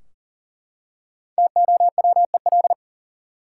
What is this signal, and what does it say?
Morse code, heard as a keyed tone, about 40 wpm, 700 Hz. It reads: TOWEL
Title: Five Letter Words - 40wpm